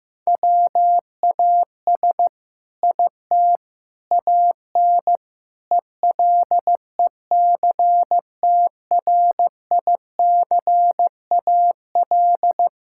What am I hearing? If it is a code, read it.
Morse code, heard as a keyed tone, about 15 wpm, 700 Hz. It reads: WAS IT AN ELECTRICAL
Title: Sentences from Top 1000 Words - Review Part 6 - 15wpm